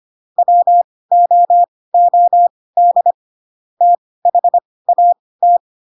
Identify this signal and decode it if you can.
Morse code, heard as a keyed tone, about 25 wpm, spaced as for 18 wpm, 700 Hz. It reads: WOOD THAT
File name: Pinocchio / Pinocchio ch1 12 25 ik1hge